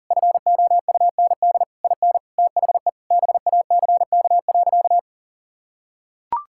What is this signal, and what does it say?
Morse code, heard as a keyed tone, about 40 wpm, 700 Hz. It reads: FOUND IN THE BACK.
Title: Sentences from Top 1000 Words - Review Part 5 - 40wpm